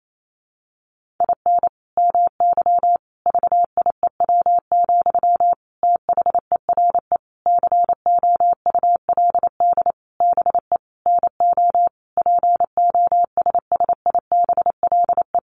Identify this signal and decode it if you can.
Morse code, heard as a keyed tone, about 28 wpm, 700 Hz. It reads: IN MY VIEW, THERE COULD BE NO POSSIBLE
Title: Sentences from Top 1500 Words 28wpm